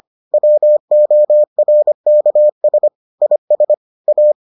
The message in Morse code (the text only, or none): WORKS IS A